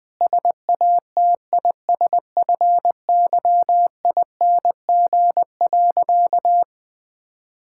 Morse code, heard as a keyed tone, about 20 wpm, 700 Hz. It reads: SATISFYING.
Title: Sentences with 3-Word Phrases - Spoken Beginning - 20wpm